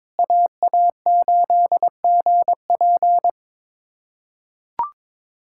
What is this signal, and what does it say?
Morse code, heard as a keyed tone, about 22 wpm, 700 Hz. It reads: AA8GP
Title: USA Callsign DE Callsign - 5 Characters - 22wpm